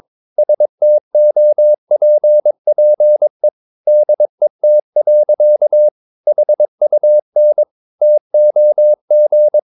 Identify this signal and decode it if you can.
Morse code, heard as a keyed tone, about 22 wpm, 600 Hz. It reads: STOPPE DET. HUN TOG
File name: Ferie QRS 22wpm